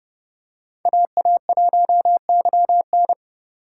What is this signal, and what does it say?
Morse code, heard as a keyed tone, about 31 wpm, 700 Hz. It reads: AA1YN
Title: USA Call Signs - 5 Characters - 3x Character Spacing - 30wpm